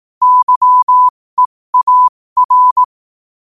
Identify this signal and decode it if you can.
Morse code, heard as a keyed tone, about 18 wpm, spaced as for 15 wpm, 1000 Hz. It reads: YEAR
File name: qotdfiles / 15wpm